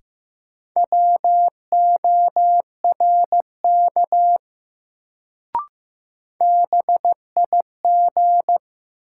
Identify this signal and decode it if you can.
Morse code, heard as a keyed tone, about 15 wpm, 700 Hz. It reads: WORK BIG
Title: Sentences from Top 200 Words - New Word Drill - 15wpm